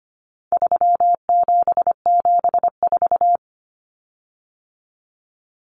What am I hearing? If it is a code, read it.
Morse code, heard as a keyed tone, about 25 wpm, 700 Hz. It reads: 3774
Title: Common Cut Numbers - 4-Digit 25wpm